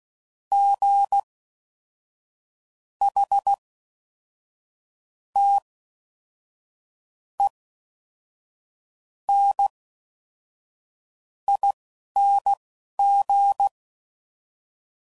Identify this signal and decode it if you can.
Morse code, heard as a keyed tone, about 16 wpm, spaced as for 5 wpm, 800 Hz. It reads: GHTENING